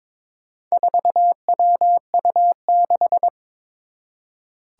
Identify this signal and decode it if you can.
Morse code, heard as a keyed tone, about 22 wpm, 700 Hz. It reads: 4WU6
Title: Four Challenging Letter-Number Combinations 22wpm